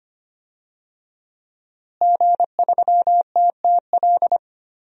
Morse code, heard as a keyed tone, about 25 wpm, 700 Hz. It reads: G3TTL
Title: DXCC-100 Call Signs - 5 Characters - Repeated 3x - 25wpm